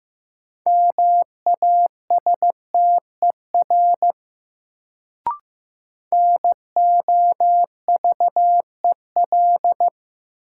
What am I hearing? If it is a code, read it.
Morse code, heard as a keyed tone, about 15 wpm, 700 Hz. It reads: MASTER NOVEL
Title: Sentences from Top 1300 Words - New Word Drill Alt - 15wpm